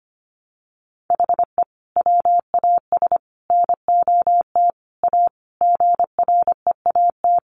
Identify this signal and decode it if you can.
Morse code, heard as a keyed tone, about 25 wpm, 700 Hz. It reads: HE WAS NOT A GREAT
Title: Sentences from Top 1200 Words 25wpm